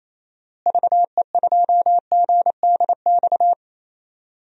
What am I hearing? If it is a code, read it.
Morse code, heard as a keyed tone, about 28 wpm, 700 Hz. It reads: VE2GDX
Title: DXCC-100 Call Signs - 6 Characters - Repeated 3x - 28wpm